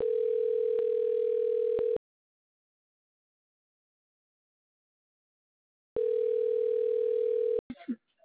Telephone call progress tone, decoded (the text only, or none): ringback tone